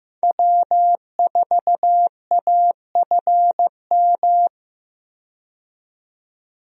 Morse code, heard as a keyed tone, about 15 wpm, 700 Hz. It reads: W4AFM
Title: USA Call Signs - 5 Characters - 15wpm